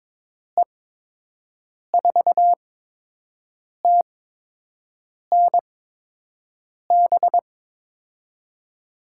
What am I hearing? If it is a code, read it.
Morse code, heard as a keyed tone, about 22 wpm, spaced as for 6 wpm, 700 Hz. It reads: E4TNB
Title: USA Call Signs - 6 Characters - 8x Character Spacing - 22wpm